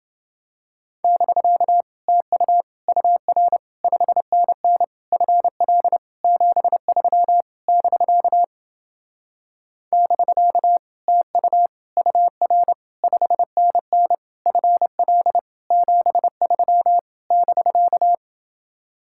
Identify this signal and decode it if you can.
Morse code, heard as a keyed tone, about 29 wpm, 700 Hz. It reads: <BK> TU UR 5NN FL 73 <BK> <BK> TU UR 5NN FL 73 <BK>